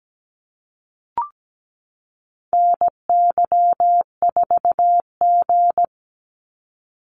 Morse code, heard as a keyed tone, about 17 wpm, 700 Hz. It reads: NY4G
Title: NA SOTA Activator Callsigns 17wpm